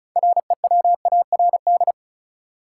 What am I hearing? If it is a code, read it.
Morse code, heard as a keyed tone, about 36 wpm, 700 Hz. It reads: REWARD
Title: Six Letter Words - 35wpm